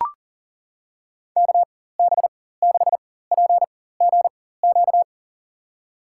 Morse code, heard as a keyed tone, about 40 wpm, spaced as for 19 wpm, 700 Hz. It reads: KB6PGQ
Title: USA Call Signs - 6 Characters - 4x Character Spacing - 40wpm